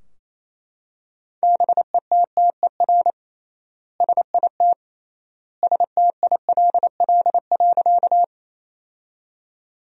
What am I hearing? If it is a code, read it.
Morse code, heard as a keyed tone, about 28 wpm, 700 Hz. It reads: BETTER SIT STILL.
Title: 3x Word-Spacing - Sentences from Top 500 Words - 28wpm